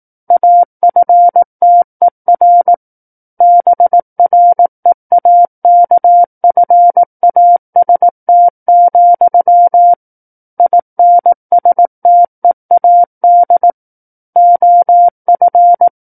Morse code, heard as a keyed tone, about 18 wpm, 700 Hz. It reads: AFTER BREAKFAST, INSTEAD OF
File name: 18 WPM